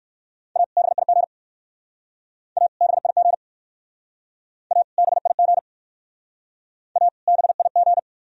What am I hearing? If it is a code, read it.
Morse code, heard as a keyed tone, about 54 wpm, 700 Hz. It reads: A BIG A BIG A BIG A BIG
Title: Speed-Racing - Sets of 2 Words 45wpm